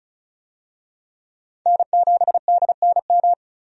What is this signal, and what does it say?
Morse code, heard as a keyed tone, about 34 wpm, 700 Hz. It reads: N7DNM
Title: USA Call Signs - 5 Characters - 35wpm